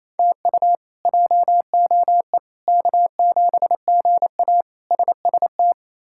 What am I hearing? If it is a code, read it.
Morse code, heard as a keyed tone, about 28 wpm, 700 Hz. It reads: TU JOE K7GA SST
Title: K1USN SST QSO Practice - 28wpm